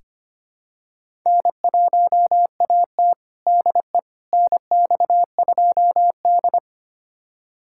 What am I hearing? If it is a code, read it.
Morse code, heard as a keyed tone, about 25 wpm, 700 Hz. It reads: N1AT DE NX2D
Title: USA Callsign DE Callsign - 4 Characters - 25wpm